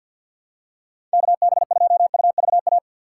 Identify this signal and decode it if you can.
Morse code, heard as a keyed tone, about 49 wpm, 700 Hz. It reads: KB2UVA
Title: USA Call Signs - 6 Characters - 6x Character Spacing - 50wpm